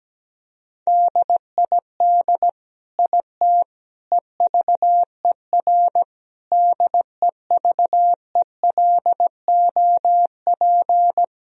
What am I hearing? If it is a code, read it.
Morse code, heard as a keyed tone, about 17 wpm, 700 Hz. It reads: DID IT EVER DEVELOP
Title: Sentences from Top 1000 Words - Review Part 2 - 17wpm